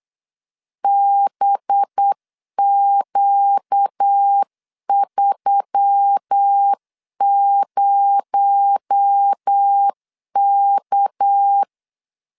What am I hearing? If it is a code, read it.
Morse code, heard as a keyed tone, about 8 wpm, 800 Hz. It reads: BQ30K
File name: CW / DET / 1 6 1500C S C 8wpm 5219